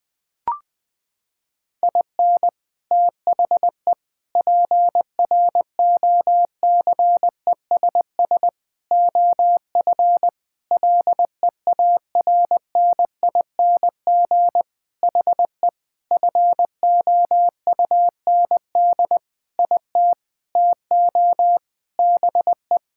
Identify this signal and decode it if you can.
Morse code, heard as a keyed tone, about 21 wpm, 700 Hz. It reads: IN THE PROCESS OF LEARNING HE FOUND IT TO BE
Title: Sentences with 4-Word Phrases - 20wpm